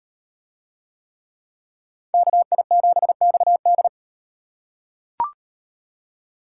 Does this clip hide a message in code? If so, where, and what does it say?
Yes: Morse code from 2.1 to 3.9 s, heard as a keyed tone, about 38 wpm, 700 Hz. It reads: KI7XD